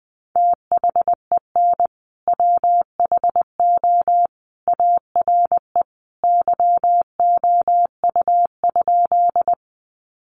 Morse code, heard as a keyed tone, about 20 wpm, 700 Hz. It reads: THEN WHO ARE YOU?
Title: Sentences from Top 100 Words 20wpm